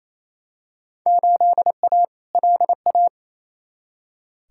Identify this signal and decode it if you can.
Morse code, heard as a keyed tone, about 28 wpm, 700 Hz. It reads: 8A LA
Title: Field Day Exchanges Encore - 28wpm